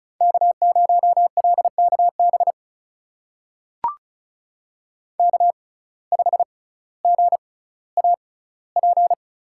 Morse code, heard as a keyed tone, about 36 wpm, spaced as for 18 wpm, 700 Hz. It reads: K0LKB K5GAP
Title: USA Call Signs - 5 Characters - 6x Character Spacing - 35wpm